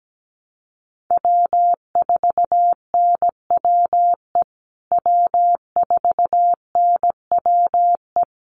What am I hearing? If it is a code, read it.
Morse code, heard as a keyed tone, about 17 wpm, 700 Hz. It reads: W4NWE W4NWE
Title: USA Call Signs Repeated 2x 17wpm